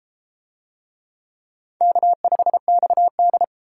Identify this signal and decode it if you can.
Morse code, heard as a keyed tone, about 31 wpm, 700 Hz. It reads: K5XD